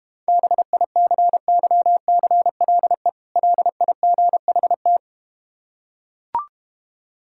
Morse code, heard as a keyed tone, about 32 wpm, 690 Hz. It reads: BICYCLE LIGHT